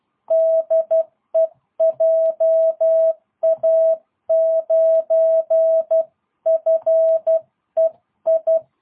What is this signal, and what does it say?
Morse code, heard as a keyed tone, about 12 wpm, 660 Hz. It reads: DE JA9FEI
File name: jh8cbh / 2022cw